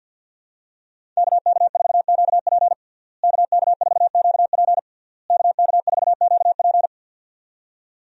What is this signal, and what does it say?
Morse code, heard as a keyed tone, about 50 wpm, 700 Hz. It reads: KK4QP KK4QP KK4QP
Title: USA Call Signs - 5 Characters - Repeated 3x - 50wpm